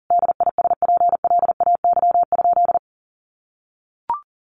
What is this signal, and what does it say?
Morse code, heard as a keyed tone, about 40 wpm, 700 Hz. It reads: DISPLAY?
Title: Sentences from Top 1200 Words 40wpm